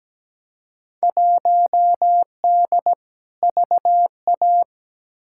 Morse code, heard as a keyed tone, about 17 wpm, 700 Hz. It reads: 1D VA
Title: Field Day Exchanges - 17wpm